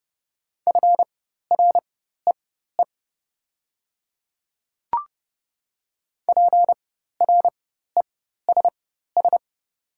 Morse code, heard as a keyed tone, about 30 wpm, spaced as for 14 wpm, 700 Hz. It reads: FREE PRESS
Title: 4x Character Spacing - Top 500 Words - 30wpm